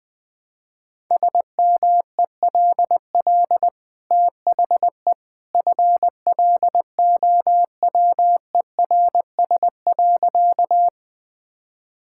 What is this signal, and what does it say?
Morse code, heard as a keyed tone, about 20 wpm, 700 Hz. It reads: SMELL THE FLOWERS.
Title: Easy Sentences with Hints - 20wpm